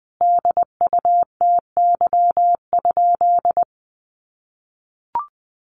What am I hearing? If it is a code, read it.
Morse code, heard as a keyed tone, about 20 wpm, 700 Hz. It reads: DUTY?
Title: Sentences from Top 1500 Words 20wpm